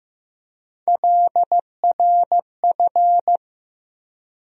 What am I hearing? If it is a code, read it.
Morse code, heard as a keyed tone, about 16 wpm, 700 Hz. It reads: LRF
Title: USA Call Sign Postfixes - 3 Letters 15wpm